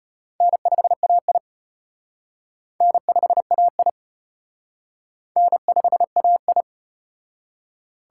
Morse code, heard as a keyed tone, about 33 wpm, 700 Hz. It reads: N5AI N5AI N5AI